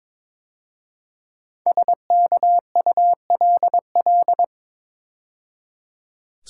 Morse code, heard as a keyed tone, about 22 wpm, 700 Hz. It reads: SKULL
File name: 5 Letter Words - First 3 Letters Given - 22wpm